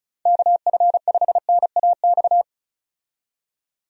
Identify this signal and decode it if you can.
Morse code, heard as a keyed tone, about 34 wpm, 700 Hz. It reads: KF5NAX
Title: USA Call Signs - 6 Characters - 35wpm